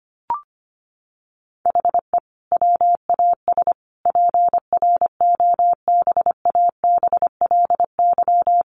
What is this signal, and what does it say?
Morse code, heard as a keyed tone, about 25 wpm, 700 Hz. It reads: HE WAS PROBABLY
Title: Sentences from Top 1300 Words 25wpm